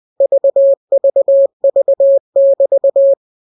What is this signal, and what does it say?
Morse code, heard as a keyed tone, about 20 wpm, 550 Hz. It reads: VVV<BT>
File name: jokes / joke 4 20 0000